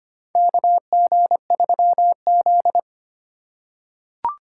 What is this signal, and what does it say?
Morse code, heard as a keyed tone, about 25 wpm, 700 Hz. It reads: KG3Z
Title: USA Call Signs - 2x Character Spacing - 25wpm